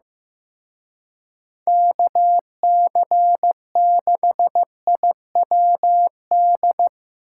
Morse code, heard as a keyed tone, about 17 wpm, 700 Hz. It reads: KC6IWD
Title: Speed Racing - USA 2x3 Call Signs 15wpm